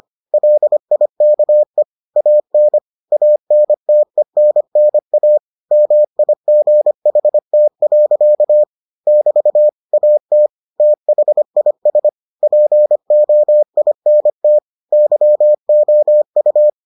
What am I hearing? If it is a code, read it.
Morse code, heard as a keyed tone, about 25 wpm, 600 Hz. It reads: LIKE AN ANTENNA MIGHT. <BT> AT THIS POINT YOU